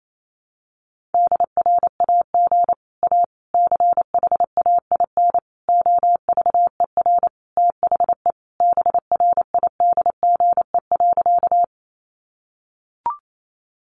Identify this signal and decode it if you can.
Morse code, heard as a keyed tone, about 28 wpm, 700 Hz. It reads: DRAG A CHAIN OVER THE BRIDGE.
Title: Sentences from Top 1500 Words 28wpm